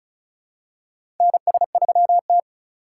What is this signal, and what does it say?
Morse code, heard as a keyed tone, about 36 wpm, 700 Hz. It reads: NS3T
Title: USA Call Signs - 4 Characters - 2x Character Spacing - 35wpm